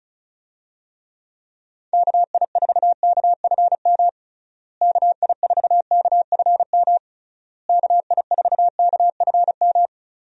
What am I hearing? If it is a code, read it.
Morse code, heard as a keyed tone, about 36 wpm, 700 Hz. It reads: KI4KFM KI4KFM KI4KFM
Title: USA Call Signs - 6 Characters - Repeated 3x - 35wpm